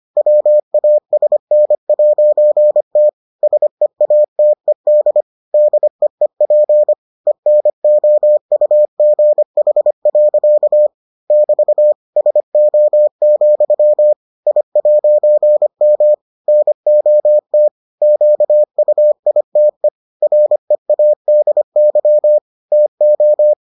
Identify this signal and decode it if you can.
Morse code, heard as a keyed tone, about 25 wpm, 600 Hz. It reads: WASN'T SEATED DEEP ENOUGH. <BT> SO, I'M NOT QUITE READY TO